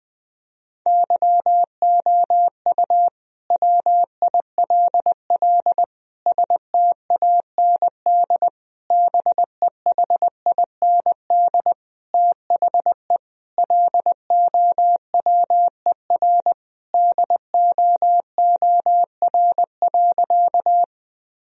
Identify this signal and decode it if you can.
Morse code, heard as a keyed tone, about 20 wpm, 700 Hz. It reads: YOU WILL STAND BEHIND THE LOWER DOOR.
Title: Sentences from Top 1000 Words - Review Part 4 - 20wpm